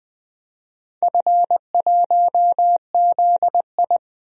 Morse code, heard as a keyed tone, about 20 wpm, 700 Hz. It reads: F1ZI
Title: DXCC-100 Call Signs - 4 Characters - 20wpm